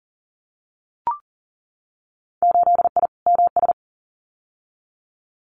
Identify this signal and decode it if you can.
Morse code, heard as a keyed tone, about 38 wpm, 700 Hz. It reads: 8I MS
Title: Winter Field Day Exchanges Encore 40wpm